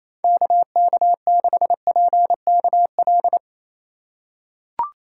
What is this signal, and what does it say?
Morse code, heard as a keyed tone, about 28 wpm, 700 Hz. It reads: KK6PKL
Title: USA Call Signs - 3x Character Spacing - 28wpm